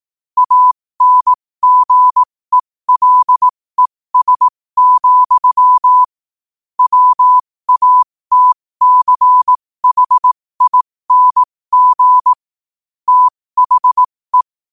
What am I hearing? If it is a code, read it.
Morse code, heard as a keyed tone, about 18 wpm, spaced as for 15 wpm, 1000 Hz. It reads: ANGELES, WATCHING THE
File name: qotdfiles / 15wpm